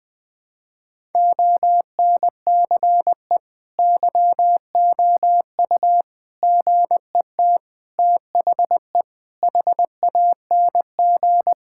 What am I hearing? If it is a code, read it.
Morse code, heard as a keyed tone, about 20 wpm, 700 Hz. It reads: ONCE YOU GET THE HANG
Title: Sentences with 7-Word Phrases - Spoken Beginning - 20wpm